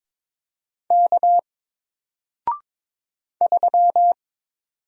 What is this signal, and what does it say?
Morse code, heard as a keyed tone, about 22 wpm, 700 Hz. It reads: K 3
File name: Single-Letter-Number-22wpm